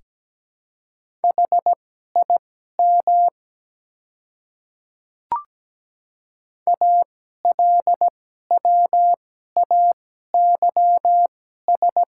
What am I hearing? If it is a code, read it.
Morse code, heard as a keyed tone, about 17 wpm, spaced as for 12 wpm, 700 Hz. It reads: HIM ALWAYS
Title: 2x Character Spacing - Top 300 Words - 17wpm